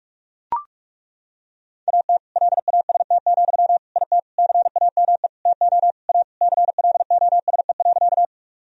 Morse code, heard as a keyed tone, about 41 wpm, 700 Hz. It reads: AT LAST, IT CAME TO A CLOSE.
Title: Sentences from Top 1000 Words - Review Part 1 - 45wpm